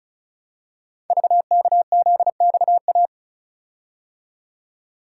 Morse code, heard as a keyed tone, about 36 wpm, 700 Hz. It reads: VKZXA
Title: Code Groups - 5 Letters - 35wpm